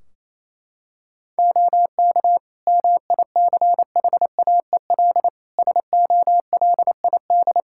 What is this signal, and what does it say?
Morse code, heard as a keyed tone, about 28 wpm, 700 Hz. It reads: OK MICHAEL SOLID
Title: K1USN SST QSO Practice 28wpm